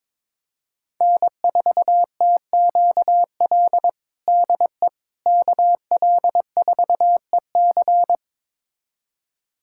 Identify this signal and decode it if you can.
Morse code, heard as a keyed tone, about 22 wpm, 700 Hz. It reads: N4TQL DE KL4EC
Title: USA Callsign DE Callsign - 5 Characters - 22wpm